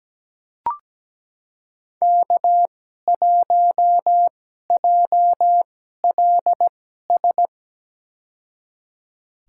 Morse code, heard as a keyed tone, about 17 wpm, spaced as for 12 wpm, 700 Hz. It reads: K1JLS
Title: USA Call Signs - 5 Characters - 2x Character Spacing - 17wpm